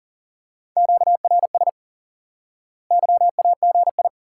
Morse code, heard as a keyed tone, about 40 wpm, 700 Hz. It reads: QRS YAGI
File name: IQE Course 3 - Lesson 89 - YAGI - Two Elements - 40wpm